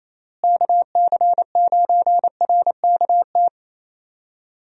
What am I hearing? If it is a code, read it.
Morse code, heard as a keyed tone, about 28 wpm, 700 Hz. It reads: KC9RKT
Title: USA Call Signs - 6 Characters - Repeated 2x - 28wpm